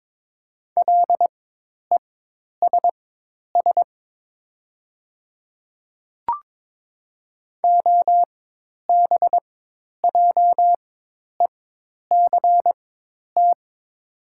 Morse code, heard as a keyed tone, about 21 wpm, spaced as for 10 wpm, 700 Hz. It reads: LESS OBJECT
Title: 4x Character Spacing - Top 500 Words - 22wpm